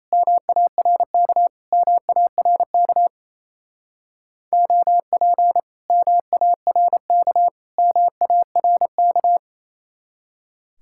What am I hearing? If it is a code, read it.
Morse code, heard as a keyed tone, about 33 wpm, 700 Hz. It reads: MARK MARK OP MARK MARK
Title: Speed-Racing Common QSO Phrases Encore - 28wpm